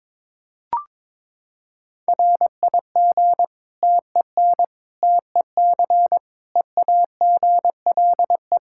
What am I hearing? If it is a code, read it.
Morse code, heard as a keyed tone, about 21 wpm, 700 Hz. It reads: RIG TEN TEC EAGLE
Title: Practice QSO - Exchange 2 - Info - 22wpm